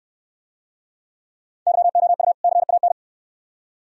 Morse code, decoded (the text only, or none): YOU CAN